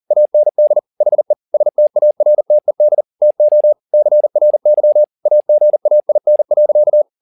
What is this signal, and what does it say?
Morse code, heard as a keyed tone, about 40 wpm, 600 Hz. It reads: AND HE STARTED TO CRY AGAIN.
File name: the old man and the sea 2 40wpm